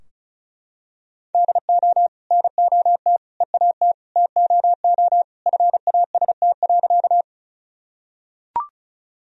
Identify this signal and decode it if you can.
Morse code, heard as a keyed tone, about 36 wpm, 700 Hz. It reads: DO NOT EAT TOO FAST.
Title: Sentences from Top 1000 Words - Review Part 2 - 35wpm